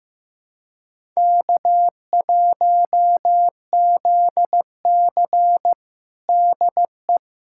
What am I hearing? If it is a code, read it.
Morse code, heard as a keyed tone, about 15 wpm, 700 Hz. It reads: K1ZC DE
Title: USA Callsign DE Callsign - 4 Characters - 15wpm